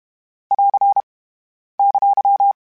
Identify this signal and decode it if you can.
Morse code, heard as a keyed tone, about 32 wpm, 800 Hz. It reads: <AR> !
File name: Special INGs-mp3s / Special INGs-mp3s / Dble ings @30